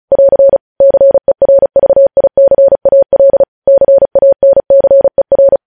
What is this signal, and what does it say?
Morse code, heard as a keyed tone, about 36 wpm, 550 Hz. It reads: <AR> CERVICAL CANCER
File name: podcast / News headlines-1x-35x35